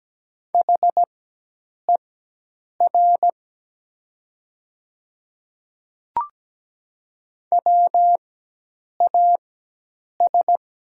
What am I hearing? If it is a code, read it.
Morse code, heard as a keyed tone, about 17 wpm, spaced as for 8 wpm, 700 Hz. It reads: HER WAS
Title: 4x Character Spacing - Top 100 Words - 17wpm